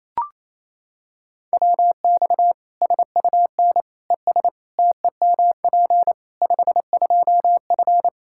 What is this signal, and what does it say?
Morse code, heard as a keyed tone, about 28 wpm, 700 Hz. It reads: WX SUN ES TEMP 52F
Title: Practice QSO - Exchange 2 - Info - 28wpm